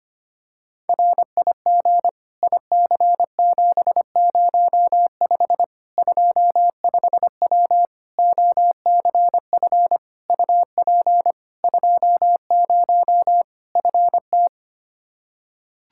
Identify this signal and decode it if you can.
Morse code, heard as a keyed tone, about 25 wpm, 700 Hz. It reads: RIG IC705 25W OCF UP 20 FT
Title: K1USN SST QSO Practice 25wpm